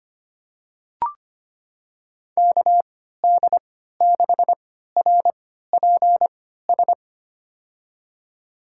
Morse code, heard as a keyed tone, about 25 wpm, spaced as for 14 wpm, 700 Hz. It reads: KD6RPS
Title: USA Call Signs - 6 Characters - 3x Character Spacing - 25wpm